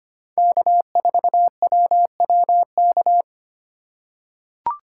Morse code, heard as a keyed tone, about 25 wpm, 700 Hz. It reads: K4WWK
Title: USA Call Signs - 5 Characters - 3x Character Spacing - 25wpm